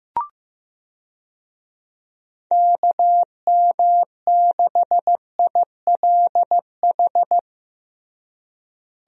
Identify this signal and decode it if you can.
Morse code, heard as a keyed tone, about 15 wpm, 700 Hz. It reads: KM6ILH
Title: USA Call Signs - 6 Characters - Repeated 2x - 15wpm